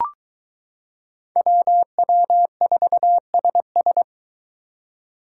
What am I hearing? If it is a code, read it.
Morse code, heard as a keyed tone, about 22 wpm, 700 Hz. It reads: WW4SS